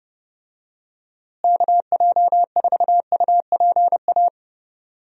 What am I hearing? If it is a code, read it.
Morse code, heard as a keyed tone, about 29 wpm, 700 Hz. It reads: KJ4UPA
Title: USA Call Signs - 6 Characters - Repeated 2x - 30wpm